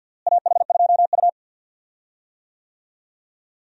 Morse code, heard as a keyed tone, about 46 wpm, 700 Hz. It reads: AH2U
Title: USA Call Signs - 4 Characters - Repeated 3x - 50wpm